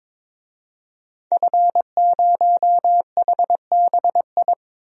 Morse code, heard as a keyed tone, about 22 wpm, 700 Hz. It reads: F0HBI
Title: DXCC-100 Call Signs - 5 Characters - 22wpm